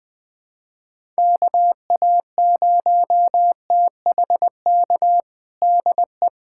Morse code, heard as a keyed tone, about 21 wpm, 700 Hz. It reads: KA0THK DE
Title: USA Callsign DE Callsign 20wpm